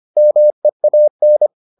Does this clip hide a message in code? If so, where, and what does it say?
Yes: Morse code from 0.2 to 1.5 s, heard as a keyed tone, about 25 wpm, 600 Hz. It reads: MEAN